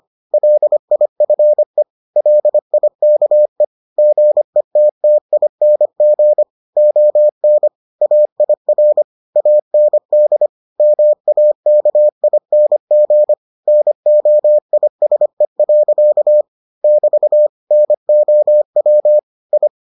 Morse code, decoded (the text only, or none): LIFE LIKE GETTING ON AIR AND MAKING NOISE. <BT> NOW I